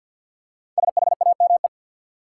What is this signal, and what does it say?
Morse code, heard as a keyed tone, about 52 wpm, 700 Hz. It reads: SHAME
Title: Five Letter Words - 50wpm